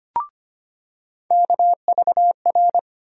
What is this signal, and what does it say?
Morse code, heard as a keyed tone, about 25 wpm, 700 Hz. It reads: KVR
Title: USA Call Sign Postfixes - 3 Letters 25wpm